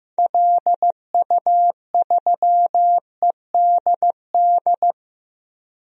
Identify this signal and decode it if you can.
Morse code, heard as a keyed tone, about 15 wpm, 700 Hz. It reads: LU3EDD
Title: DXCC-100 Call Signs - 6 Characters - Repeated 2x - 15wpm